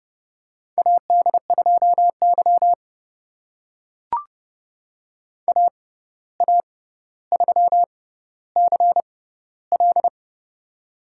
Morse code, heard as a keyed tone, about 30 wpm, spaced as for 14 wpm, 700 Hz. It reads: AD2Y AA3CL
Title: USA Call Signs - 6x Character Spacing - 30wpm